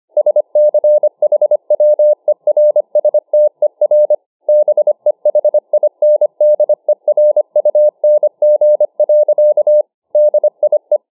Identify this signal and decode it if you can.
Morse code, heard as a keyed tone, about 25 wpm, 600 Hz. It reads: SCHWERSTER BEHINDERUNG. DIE